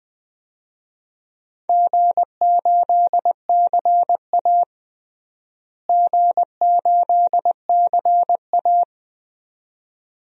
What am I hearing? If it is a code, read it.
Morse code, heard as a keyed tone, about 20 wpm, 700 Hz. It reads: G8CA G8CA
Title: DXCC-100 Call Signs - 4 Characters - Repeated 2x - 20wpm